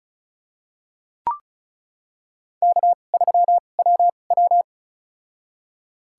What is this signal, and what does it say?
Morse code, heard as a keyed tone, about 34 wpm, spaced as for 25 wpm, 700 Hz. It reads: K3WW
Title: USA Call Signs - 2x Character Spacing - 35wpm